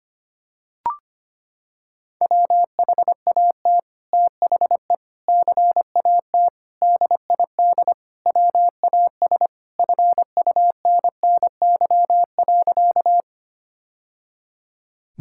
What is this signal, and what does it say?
Morse code, heard as a keyed tone, about 25 wpm, 690 Hz. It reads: WHAT THE CAT DID WAS FUNNY.
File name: practice / 25wpm / Sentences with 2-Word Phrases - 25wpm